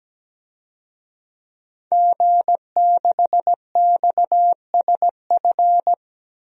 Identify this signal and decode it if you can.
Morse code, heard as a keyed tone, about 17 wpm, 700 Hz. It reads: G6XSF
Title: DXCC-100 Call Signs - 5 Characters - 17wpm